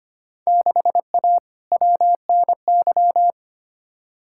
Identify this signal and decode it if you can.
Morse code, heard as a keyed tone, about 25 wpm, 700 Hz. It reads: 6A WNY
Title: Field Day Exchanges Encore - 25wpm